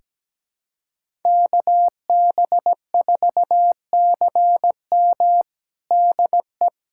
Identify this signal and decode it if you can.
Morse code, heard as a keyed tone, about 17 wpm, 700 Hz. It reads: KB4CM DE
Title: USA Callsign DE Callsign - 5 Characters - 17wpm